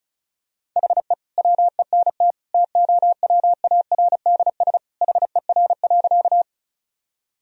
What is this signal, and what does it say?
Morse code, heard as a keyed tone, about 36 wpm, 700 Hz. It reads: HE WENT TOWARDS HER.
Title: Sentences from Top 1000 Words - Review Part 2 - 35wpm